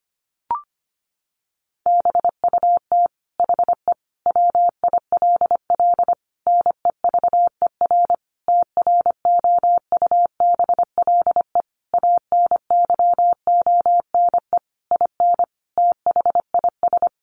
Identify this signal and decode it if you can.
Morse code, heard as a keyed tone, about 25 wpm, 700 Hz. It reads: BUT HE WILL NEVER TROUBLE ANYONE IN THIS
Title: Sentences from Top 1200 Words 25wpm